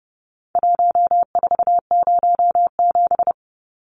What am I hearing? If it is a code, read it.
Morse code, heard as a keyed tone, about 30 wpm, 700 Hz. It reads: 1407
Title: Numbers - Contest Style - 4-Digit 30wpm